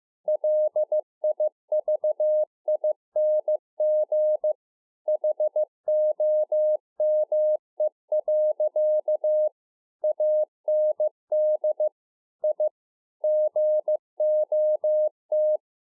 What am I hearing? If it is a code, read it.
Morse code, heard as a keyed tone, about 15 wpm, 610 Hz. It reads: LIVING HOME. AND I GOT